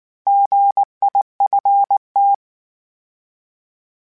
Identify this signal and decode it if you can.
Morse code, heard as a keyed tone, about 19 wpm, 800 Hz. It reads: GIFT